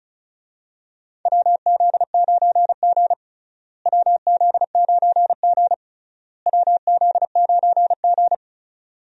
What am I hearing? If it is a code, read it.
Morse code, heard as a keyed tone, about 34 wpm, 700 Hz. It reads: WZ9G WZ9G WZ9G
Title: USA Call Signs - 4 Characters - Repeated 3x - 35wpm